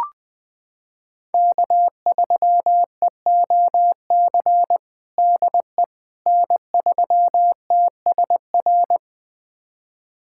Morse code, heard as a keyed tone, about 20 wpm, 700 Hz. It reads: K3EOC DE N3TSR
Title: USA Callsign DE Callsign - 5 Characters - 20wpm